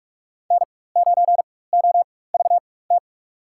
Morse code, heard as a keyed tone, about 47 wpm, spaced as for 21 wpm, 700 Hz. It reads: N9OVT
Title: USA Call Signs - 4x Character Spacing - 45wpm